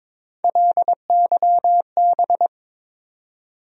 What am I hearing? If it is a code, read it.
Morse code, heard as a keyed tone, about 21 wpm, 700 Hz. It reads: LYB
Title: USA Call Sign Postfixes - 3 Letters 22wpm